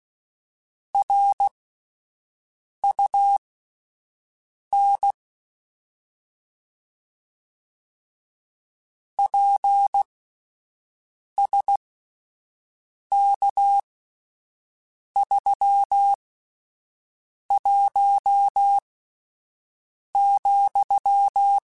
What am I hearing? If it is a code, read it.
Morse code, heard as a keyed tone, about 16 wpm, spaced as for 5 wpm, 800 Hz. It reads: RUN PSK31,
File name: QSOs Spaced